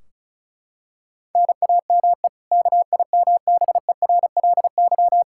Morse code, heard as a keyed tone, about 36 wpm, 700 Hz. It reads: NAME KIMBERLY
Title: Practice QSO - Exchange 1 - Beginning - 35wpm